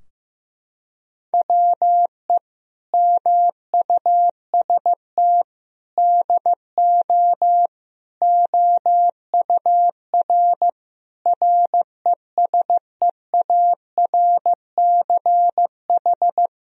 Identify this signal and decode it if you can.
Morse code, heard as a keyed tone, about 15 wpm, 700 Hz. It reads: WE MUST DO OUR RESEARCH
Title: Sentences from Top 1000 Words - Review Part 4 - 15wpm